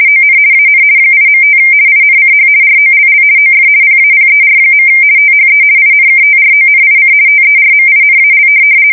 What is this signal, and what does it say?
RTTY, about 50 baud. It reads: ON AN 8.5 INCH ROLL OF PAPER. THE INKTRONIC TERMINAL
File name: fsk170-50B / teletype-corp